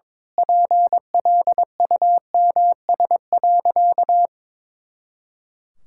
Morse code, heard as a keyed tone, about 22 wpm, 700 Hz. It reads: PLUMS.
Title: Pangram Challenge v2 - 22wpm